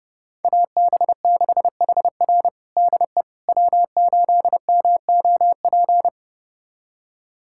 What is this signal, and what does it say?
Morse code, heard as a keyed tone, about 30 wpm, 700 Hz. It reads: AB6HR DE W8MOP
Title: USA Callsign DE Callsign - 5 Characters - 30wpm